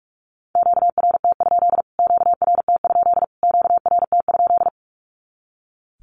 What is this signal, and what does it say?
Morse code, heard as a keyed tone, about 44 wpm, 700 Hz. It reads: QRT? QRT? QRT?
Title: Q-Codes Repeated 3x 45wpm